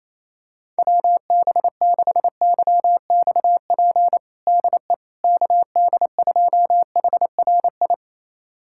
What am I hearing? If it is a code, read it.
Morse code, heard as a keyed tone, about 28 wpm, 700 Hz. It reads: WB6YXP DE KD2HRI
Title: USA Callsign DE Callsign 28wpm